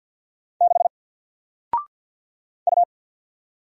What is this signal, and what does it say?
Morse code, heard as a keyed tone, about 53 wpm, 700 Hz. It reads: 6 U
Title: Single Letter-Number 50wpm